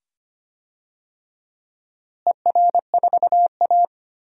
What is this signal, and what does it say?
Morse code, heard as a keyed tone, about 25 wpm, 700 Hz. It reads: ER4A
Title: DXCC-100 Call Signs - 4 Characters - Repeated 2x - 25wpm